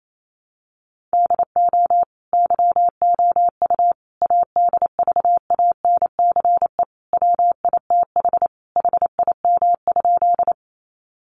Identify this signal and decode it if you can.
Morse code, heard as a keyed tone, about 28 wpm, 700 Hz. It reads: DO YOU ADVANCE WITH HIM?
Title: Sentences from Top 1400 Words 28wpm